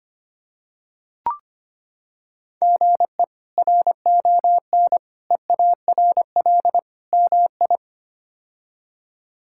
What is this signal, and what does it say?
Morse code, heard as a keyed tone, about 25 wpm, 700 Hz. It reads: GE RON EARL MI
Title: K1USN SST QSO Practice - 25wpm